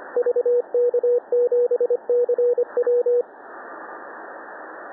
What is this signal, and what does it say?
Morse code, heard as a keyed tone, about 24 wpm, 490 Hz, with radio noise. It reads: VK7CW
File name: TN5E wkg VK7CW 20CW